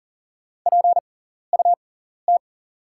Morse code, heard as a keyed tone, about 37 wpm, spaced as for 14 wpm, 700 Hz. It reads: PUT
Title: 6x Character Spacing - Top 200 Words - 40wpm